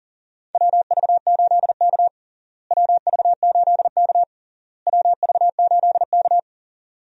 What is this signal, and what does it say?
Morse code, heard as a keyed tone, about 37 wpm, 700 Hz. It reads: WV8K WV8K WV8K
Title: USA Call Signs - 4 Characters - Repeated 3x - 40wpm